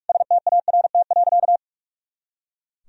Morse code, heard as a keyed tone, about 41 wpm, 700 Hz. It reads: START.
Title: Sentences from Top 300 Words and Abbreviations 45wpm